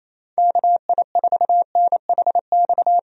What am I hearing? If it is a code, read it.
Morse code, heard as a keyed tone, about 28 wpm, 700 Hz. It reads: KI4NHX
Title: USA Call Signs - 6 Characters - Repeated 2x - 28wpm